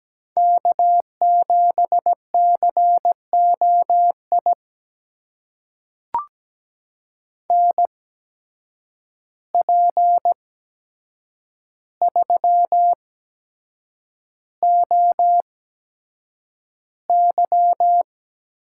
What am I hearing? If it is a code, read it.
Morse code, heard as a keyed tone, about 17 wpm, spaced as for 7 wpm, 700 Hz. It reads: K7COI NP3OY
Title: USA Call Signs - 8x Character Spacing - 17wpm